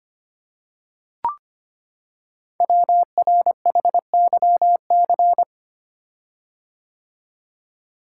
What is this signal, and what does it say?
Morse code, heard as a keyed tone, about 25 wpm, 700 Hz. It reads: WRHYC
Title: Code Groups - 5 Letters - 25wpm